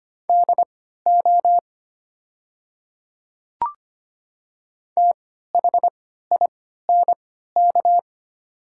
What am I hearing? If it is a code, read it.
Morse code, heard as a keyed tone, about 25 wpm, spaced as for 14 wpm, 700 Hz. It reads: DO THINK
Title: 3x Character Spacing - Top 200 Words - 25wpm